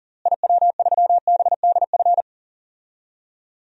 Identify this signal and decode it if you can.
Morse code, heard as a keyed tone, about 40 wpm, 700 Hz. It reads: IW3BDF
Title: DXCC-100 Call Signs - 6 Characters - Repeated 3x - 40wpm